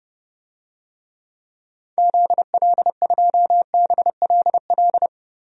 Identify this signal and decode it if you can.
Morse code, heard as a keyed tone, about 32 wpm, 690 Hz. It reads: ZL2BLL
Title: DXCC-100 Call Signs - 6 Characters - 30wpm